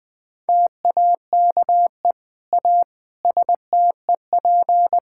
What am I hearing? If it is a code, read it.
Morse code, heard as a keyed tone, about 20 wpm, 700 Hz. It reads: TAKE A STEP
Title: Sentences with 7-Word Phrases - 20wpm